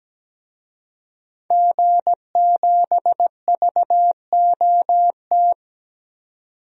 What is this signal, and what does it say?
Morse code, heard as a keyed tone, about 17 wpm, 700 Hz. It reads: G7VOT
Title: DXCC-100 Call Signs - 5 Characters - 17wpm